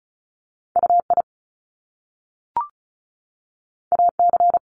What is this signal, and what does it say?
Morse code, heard as a keyed tone, about 36 wpm, 700 Hz. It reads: UI AC
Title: Top Two Letter Combinations in Words 35wpm